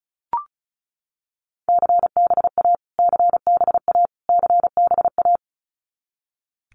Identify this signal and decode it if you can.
Morse code, heard as a keyed tone, about 34 wpm, 700 Hz. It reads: CBA CBA CBA
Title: Abbreviations Repeated 3x - 35wpm